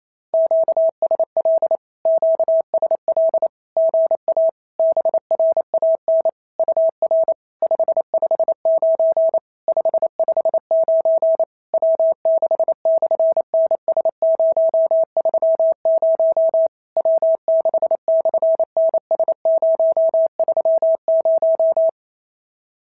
Morse code, heard as a keyed tone, about 28 wpm, 690 Hz. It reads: QSL QSL GA BRAN UR 559 559 W6/NS030 W6/NS030
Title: SOTA QSO Practice 28wpm